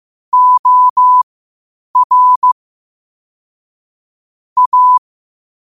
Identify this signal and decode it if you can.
Morse code, heard as a keyed tone, about 15 wpm, spaced as for 8 wpm, 1000 Hz. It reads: OR A
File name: qotdfiles / 8wpm